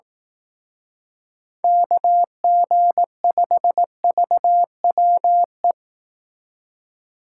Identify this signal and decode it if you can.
Morse code, heard as a keyed tone, about 18 wpm, 700 Hz. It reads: KG5VWE
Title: Speed Racing - USA Call Signs 15wpm